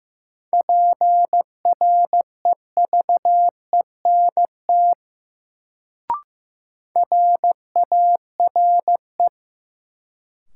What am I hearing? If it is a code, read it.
Morse code, heard as a keyed tone, about 15 wpm, 700 Hz. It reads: PREVENT RARE
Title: Sentences from Top 1100 Words - New Word Drill Alt - 15wpm